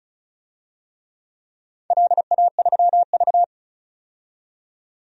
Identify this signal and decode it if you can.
Morse code, heard as a keyed tone, about 36 wpm, 700 Hz. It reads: LA3V
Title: DXCC-100 Call Signs - 4 Characters - 35wpm